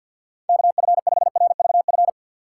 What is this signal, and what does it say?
Morse code, heard as a keyed tone, about 52 wpm, 700 Hz. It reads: KF5RVF
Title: USA Call Signs - 6 Characters - 4x Character Spacing - 50wpm